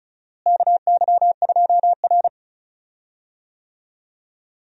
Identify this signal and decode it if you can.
Morse code, heard as a keyed tone, about 36 wpm, 700 Hz. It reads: KY2R
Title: USA Call Signs - 4 Characters - Repeated 3x - 35wpm